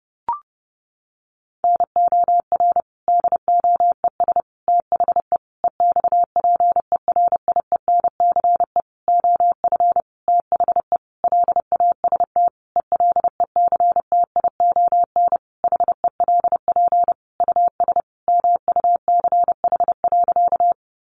Morse code, heard as a keyed tone, about 30 wpm, 700 Hz. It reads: NOR DOES THE EXPERIENCE OF THE LAST ELECTION HELP US MUCH.
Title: Sentences from Top 1500 Words 30wpm